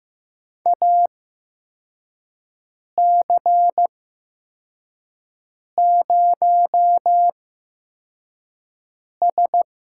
Morse code, heard as a keyed tone, about 15 wpm, spaced as for 4 wpm, 700 Hz. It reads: AC0S
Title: USA Call Signs - 8x Character Spacing - 15wpm